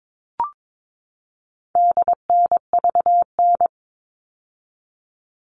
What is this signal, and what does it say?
Morse code, heard as a keyed tone, about 22 wpm, 700 Hz. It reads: DNVN
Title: All Cut Numbers - 4-Digit 22wpm